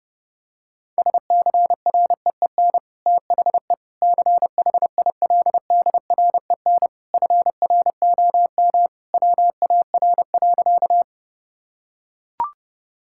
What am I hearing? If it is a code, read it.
Morse code, heard as a keyed tone, about 32 wpm, 700 Hz. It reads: SCREEN THE CHILDREN FROM WAR.
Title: Sentences from Top 1000 Words - Review Part 6 - 30wpm